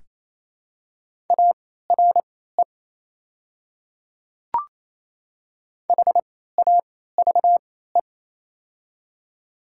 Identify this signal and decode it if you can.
Morse code, heard as a keyed tone, about 28 wpm, spaced as for 16 wpm, 700 Hz. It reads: ARE HAVE
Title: 3x Character Spacing - Top 100 Words - 28wpm